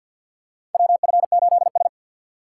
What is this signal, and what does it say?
Morse code, heard as a keyed tone, about 47 wpm, 700 Hz. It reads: WF8S
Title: USA Call Signs - 6x Character Spacing - 50wpm